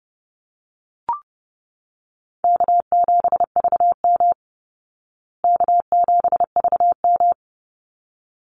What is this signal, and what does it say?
Morse code, heard as a keyed tone, about 30 wpm, 700 Hz. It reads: K7VM K7VM
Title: Easy Call Signs Repeated 2x - 30wpm